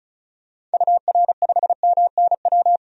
Morse code, heard as a keyed tone, about 34 wpm, 700 Hz. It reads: UR5MNW
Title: DXCC-100 Call Signs - 6 Characters - 35wpm